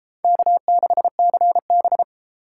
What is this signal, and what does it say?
Morse code, heard as a keyed tone, about 32 wpm, 700 Hz. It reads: K6CB